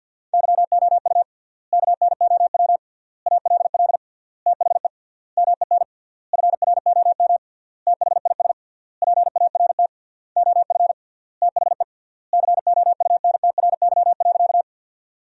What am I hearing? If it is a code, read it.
Morse code, heard as a keyed tone, about 50 wpm, 700 Hz. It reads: YOU KNOW ALL THE MEN FROM THIS PART OF THE COUNTRY.
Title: 3x Word-Spacing - Sentences from Top 300 Words - 50wpm